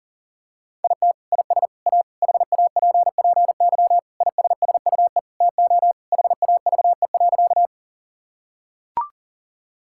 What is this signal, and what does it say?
Morse code, heard as a keyed tone, about 40 wpm, 700 Hz. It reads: IT IS A HAPPY ISSUE TO HAVE.
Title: Sentences from Top 1000 Words - Review Part 6 - 40wpm